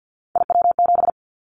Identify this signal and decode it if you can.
Morse code, heard as a keyed tone, about 50 wpm, 700 Hz. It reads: IW7
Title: Callsign Prefixes for Top 25 DXCC - 50wpm